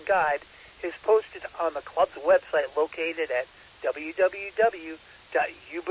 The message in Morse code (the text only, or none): none